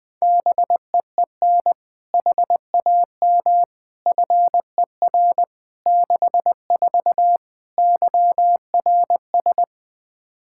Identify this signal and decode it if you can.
Morse code, heard as a keyed tone, about 20 wpm, 700 Hz. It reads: BEEN HAM FER 64 YRS
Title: K1USN SST QSO Practice 20wpm